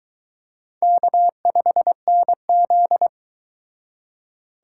Morse code, heard as a keyed tone, about 22 wpm, 700 Hz. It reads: K5NZ